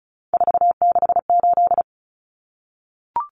Morse code, heard as a keyed tone, about 34 wpm, 700 Hz. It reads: 468
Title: Numbers - 3-Digit 35wpm